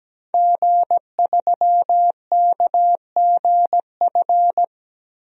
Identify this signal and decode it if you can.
Morse code, heard as a keyed tone, about 17 wpm, 700 Hz. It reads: G3KGF
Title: DXCC-100 Call Signs - 5 Characters - Repeated 2x - 17wpm